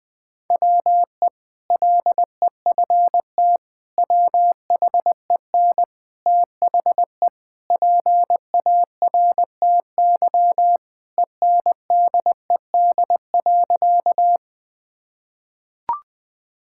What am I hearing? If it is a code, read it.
Morse code, heard as a keyed tone, about 20 wpm, 700 Hz. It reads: WE LEFT WHEN THE PARTY ENDED.
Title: Sentences with 2-Word Phrases - Spoken Beginning - 20wpm